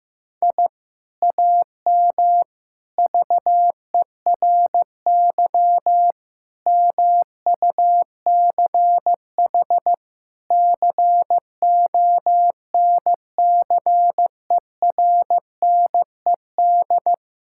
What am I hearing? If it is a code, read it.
Morse code, heard as a keyed tone, about 15 wpm, 700 Hz. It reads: I AM VERY MUCH CONCERNED
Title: Sentences from Top 1000 Words - Review Part 6 - 15wpm